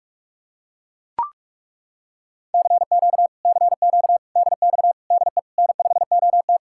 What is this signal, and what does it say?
Morse code, heard as a keyed tone, about 40 wpm, 700 Hz. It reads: CQ CQ DX DE N5OT